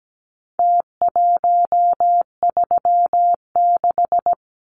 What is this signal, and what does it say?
Morse code, heard as a keyed tone, about 17 wpm, 700 Hz. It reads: T136
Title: Numbers - Contest Style - 4-Digit 17wpm